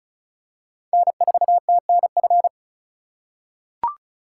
Morse code, heard as a keyed tone, about 36 wpm, 700 Hz. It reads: N4TNF
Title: USA Call Signs - 4x Character Spacing - 35wpm